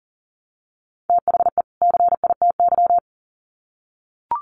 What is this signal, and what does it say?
Morse code, heard as a keyed tone, about 40 wpm, 700 Hz. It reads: THE CITY
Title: Sets of 2 Words 40wpm